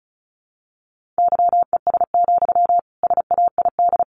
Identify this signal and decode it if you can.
Morse code, heard as a keyed tone, about 34 wpm, 700 Hz. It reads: YES, SAID
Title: Sentences from Top 500 Words 35wpm